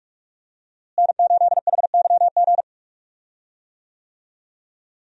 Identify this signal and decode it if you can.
Morse code, heard as a keyed tone, about 46 wpm, 700 Hz. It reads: N8HYG
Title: USA Call Signs - 5 Characters - Repeated 2x - 45wpm